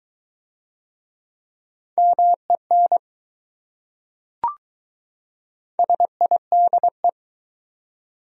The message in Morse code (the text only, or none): MEN SIDE